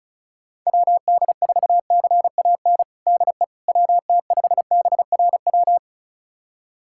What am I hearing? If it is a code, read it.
Morse code, heard as a keyed tone, about 36 wpm, 700 Hz. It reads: WD4CAN DE WT5BRW
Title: USA Callsign DE Callsign 35wpm